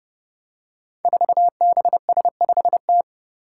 Morse code, heard as a keyed tone, about 28 wpm, 700 Hz. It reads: 4BS5T
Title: Five Challenging Letter-Number Combinations 30wpm